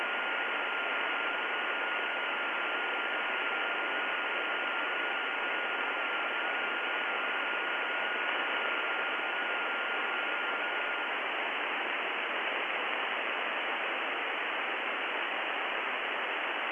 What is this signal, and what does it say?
FT8.
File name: contests / 2019-june=vhf-k5tr- -0101 0-20